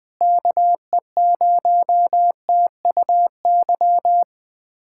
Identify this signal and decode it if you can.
Morse code, heard as a keyed tone, about 20 wpm, 700 Hz. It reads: KE0TUY
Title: USA Callsign DE Callsign - 6 Characters - 20wpm